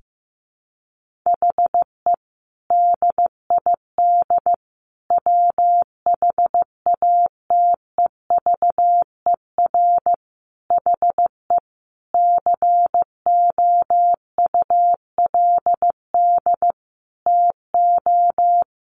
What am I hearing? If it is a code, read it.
Morse code, heard as a keyed tone, about 15 wpm, 700 Hz. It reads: HE DID WHATEVER HE COULD TO
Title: Sentences from Top 1100 Words 15wpm